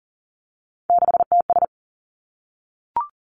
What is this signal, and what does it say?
Morse code, heard as a keyed tone, about 42 wpm, 700 Hz. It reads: 6TS
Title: Three Challenging Letter-Number Combinations 40wpm